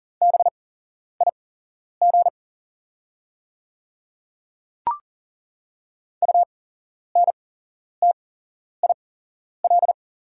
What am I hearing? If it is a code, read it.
Morse code, heard as a keyed tone, about 40 wpm, spaced as for 11 wpm, 700 Hz. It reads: BIG UNTIL
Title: 8x Character Spacing - Top 500 Words - 40wpm